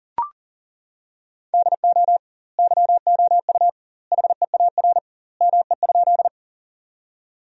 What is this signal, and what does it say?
Morse code, heard as a keyed tone, about 40 wpm, 700 Hz. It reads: DO YOU HEAR ME?
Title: 2x Word-Spacing - Sentences from Top 300 Words - 40wpm